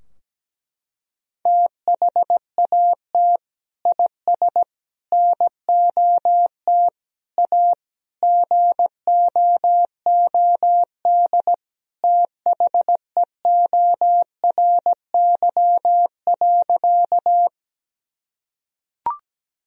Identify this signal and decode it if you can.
Morse code, heard as a keyed tone, about 17 wpm, 700 Hz. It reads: THAT IS NOT A GOOD THEORY.
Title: Sentences from Top 1000 Words - Review Part 3 - 17wpm